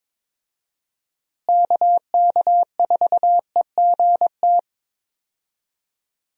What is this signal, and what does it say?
Morse code, heard as a keyed tone, about 22 wpm, 700 Hz. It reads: KK4EGT
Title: USA Call Signs - 6 Characters - Repeated 3x - 22wpm